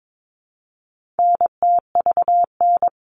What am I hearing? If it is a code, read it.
Morse code, heard as a keyed tone, about 22 wpm, 700 Hz. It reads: NTVN
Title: All Cut Numbers - 4-Digit 22wpm